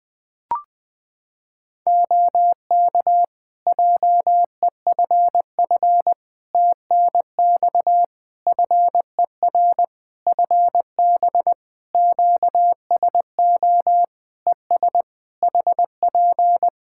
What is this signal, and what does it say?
Morse code, heard as a keyed tone, about 20 wpm, 700 Hz. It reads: OK JEFF TNX FER FB QSO ES HP
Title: Practice QSO - Exchange 4 - Ending - 20wpm